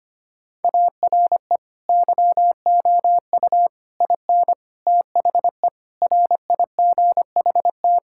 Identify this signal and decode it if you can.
Morse code, heard as a keyed tone, about 25 wpm, 700 Hz. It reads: ARE YOU IN THE RIGHT
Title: Sentences from Top 1000 Words - Review Part 6 - 25wpm